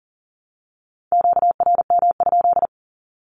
Morse code, heard as a keyed tone, about 40 wpm, 700 Hz. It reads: QRM?
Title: Q-Codes 40wpm